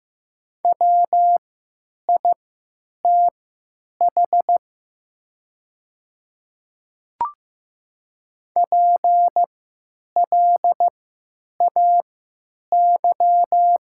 Morse code, heard as a keyed tone, about 15 wpm, spaced as for 9 wpm, 700 Hz. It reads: WITH PLAY
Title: 3x Character Spacing - Top 300 Words - 15wpm